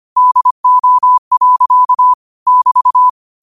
Morse code, heard as a keyed tone, about 25 wpm, 1000 Hz. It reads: DO. <BT>
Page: New quotes every day in morse code at 25 Words per minute.